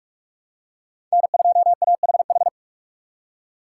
Morse code, heard as a keyed tone, about 48 wpm, 700 Hz. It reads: N2AHH
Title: USA Call Signs - 5 Characters - 6x Character Spacing - 45wpm